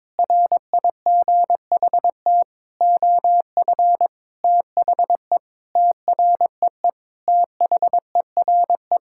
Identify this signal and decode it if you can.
Morse code, heard as a keyed tone, about 22 wpm, 700 Hz. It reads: RIGHT OF THE TREE THERE
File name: Sentences with 9-Word Phrases - 22wpm